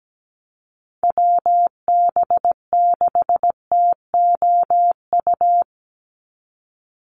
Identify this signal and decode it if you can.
Morse code, heard as a keyed tone, about 17 wpm, 700 Hz. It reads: WB6TOU
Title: Call Signs Repeated 2x - 17wpm